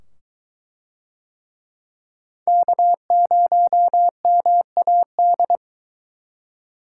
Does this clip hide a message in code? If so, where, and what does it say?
Yes: Morse code from 2.5 to 5.6 s, heard as a keyed tone, about 22 wpm, 700 Hz. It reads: K0MAD